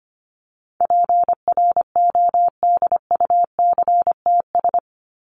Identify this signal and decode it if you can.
Morse code, heard as a keyed tone, about 25 wpm, 700 Hz. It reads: PRODUCTS
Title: ES:S Suffix - Top 100 Words with Base Words 25wpm